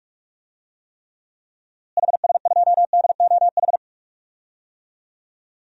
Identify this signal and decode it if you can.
Morse code, heard as a keyed tone, about 43 wpm, 700 Hz. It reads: HS2DOH
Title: DXCC-100 Call Signs - 6 Characters - Repeated 2x - 45wpm